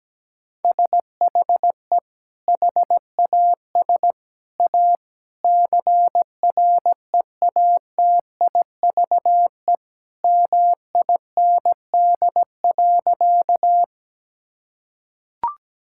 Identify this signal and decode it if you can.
Morse code, heard as a keyed tone, about 17 wpm, 700 Hz. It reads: SHE HAS A CREATIVE MIND.
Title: Sentences from Top 1000 Words - Review Part 5 - 17wpm